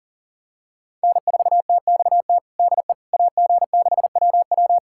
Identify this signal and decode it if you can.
Morse code, heard as a keyed tone, about 37 wpm, 700 Hz. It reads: N4TXT DE AG6WW
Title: USA Callsign DE Callsign - 5 Characters - 40wpm